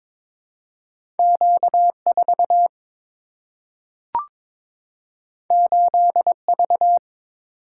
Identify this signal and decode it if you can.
Morse code, heard as a keyed tone, about 22 wpm, 700 Hz. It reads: Q4 8V
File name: practice / 22wpm / Two Letter-Number Combinations - 22wpm